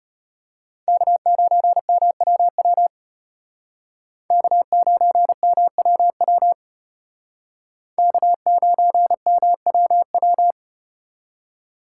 Morse code, heard as a keyed tone, about 33 wpm, 700 Hz. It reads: K9MWW K9MWW K9MWW